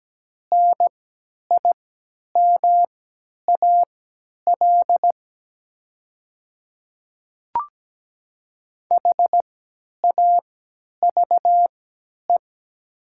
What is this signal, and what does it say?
Morse code, heard as a keyed tone, about 17 wpm, spaced as for 10 wpm, 700 Hz. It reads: NIMAL HAVE
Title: 3x Character Spacing - Top 300 Words - 17wpm